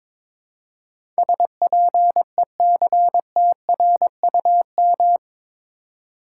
Morse code, heard as a keyed tone, about 22 wpm, 700 Hz. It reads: SPECTRUM
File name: practice / 22wpm / Eight Letter Words - 22wpm